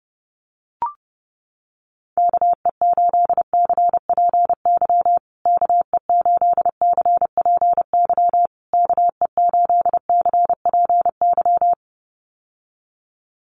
Morse code, heard as a keyed tone, about 30 wpm, 700 Hz. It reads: KE8CPY KE8CPY KE8CPY
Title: USA Call Signs Repeated 3x 30wpm